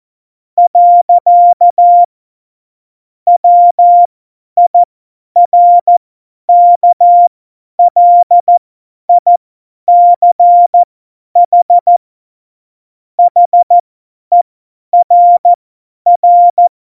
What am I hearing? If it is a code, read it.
Morse code, heard as a keyed tone, about 14 wpm, spaced as for 10 wpm, 700 Hz. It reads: . WIRKLICH HERR
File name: Morsekurs / Witze / Witz 08. Dezember 2020, 50 F70